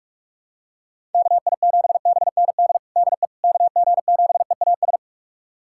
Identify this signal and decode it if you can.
Morse code, heard as a keyed tone, about 42 wpm, 700 Hz. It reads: KI7BND DE KG7EAS
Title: USA Callsign DE Callsign - 6 Characters - 45wpm